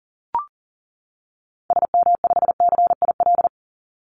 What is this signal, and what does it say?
Morse code, heard as a keyed tone, about 40 wpm, 700 Hz. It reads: SM5CIL
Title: Call Signs - 40wpm